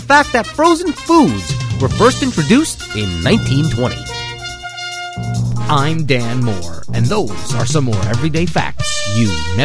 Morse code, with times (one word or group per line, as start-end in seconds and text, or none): none